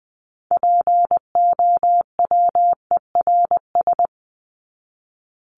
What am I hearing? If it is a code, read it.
Morse code, heard as a keyed tone, about 20 wpm, 700 Hz. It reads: POWERS
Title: ES:S Suffix - Top 100 Words with Base Words 20wpm